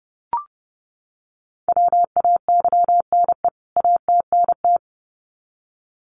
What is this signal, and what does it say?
Morse code, heard as a keyed tone, about 30 wpm, 700 Hz. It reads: WAYNE ATNT
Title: CWT Exchange 30wpm